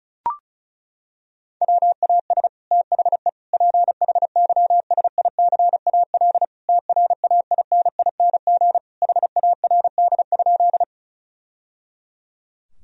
Morse code, heard as a keyed tone, about 34 wpm, 700 Hz. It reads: WAS THE PHYSICAL TRAINING HARD?
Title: Sentences from Top 1000 Words - Review Part 4 - 35wpm